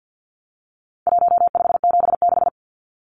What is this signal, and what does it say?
Morse code, heard as a keyed tone, about 50 wpm, 700 Hz. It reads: 1576
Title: Numbers - Contest Style - 4-Digit 50wpm